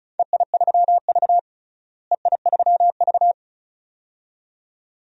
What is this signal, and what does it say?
Morse code, heard as a keyed tone, about 36 wpm, 700 Hz. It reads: EI3V EI3V
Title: DXCC-100 Call Signs - 4 Characters - Repeated 2x - 35wpm